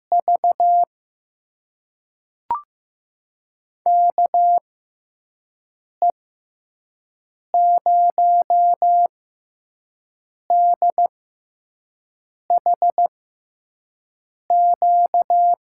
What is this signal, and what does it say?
Morse code, heard as a keyed tone, about 15 wpm, spaced as for 5 wpm, 700 Hz. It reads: V KE0DHQ
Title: USA Call Signs - 6 Characters - 6x Character Spacing - 15wpm